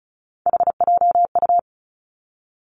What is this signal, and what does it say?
Morse code, heard as a keyed tone, about 36 wpm, 700 Hz. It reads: HJU
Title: Three Challenging Letter-Number Combinations 35wpm